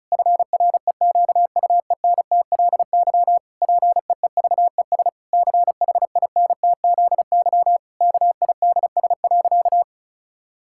Morse code, heard as a keyed tone, about 36 wpm, 700 Hz. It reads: FREQUENTLY PEEVES CHINTZY KIDS.
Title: Pangram Challenge v2 - 35wpm